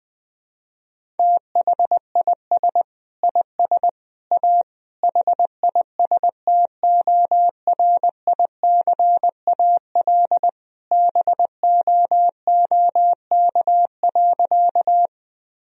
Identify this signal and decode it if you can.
Morse code, heard as a keyed tone, about 19 wpm, 700 Hz. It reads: THIS IS A HISTORICAL BOOK.
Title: Sentences from Top 1000 Words - Review Part 4 - 20wpm